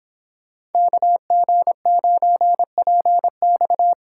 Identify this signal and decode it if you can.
Morse code, heard as a keyed tone, about 30 wpm, 700 Hz. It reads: KG9PX